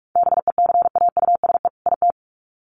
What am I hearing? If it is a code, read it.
Morse code, heard as a keyed tone, about 44 wpm, 700 Hz. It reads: BECAUSE IT
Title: Sets of 2 Words 45wpm